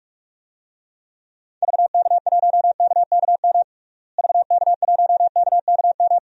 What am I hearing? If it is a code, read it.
Morse code, heard as a keyed tone, about 41 wpm, 700 Hz. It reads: VK1KKM VK1KKM
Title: DXCC-100 Call Signs - 6 Characters - Repeated 3x - 45wpm